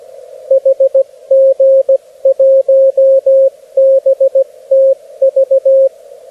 Morse code, heard as a keyed tone, about 17 wpm, 540 Hz, with radio noise. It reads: HG1BTV
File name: beacons / HG1BTV